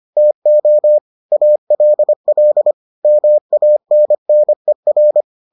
Morse code, heard as a keyed tone, about 25 wpm, 600 Hz. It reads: TO ALL MANNER